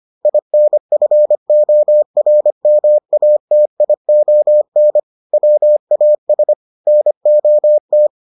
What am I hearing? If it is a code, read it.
Morse code, heard as a keyed tone, about 25 wpm, 600 Hz. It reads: INFORMATION WAS NOT